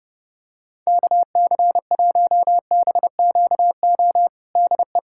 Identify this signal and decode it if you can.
Morse code, heard as a keyed tone, about 32 wpm, 700 Hz. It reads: KC1BQO DE
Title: USA Callsign DE Callsign - 6 Characters - 30wpm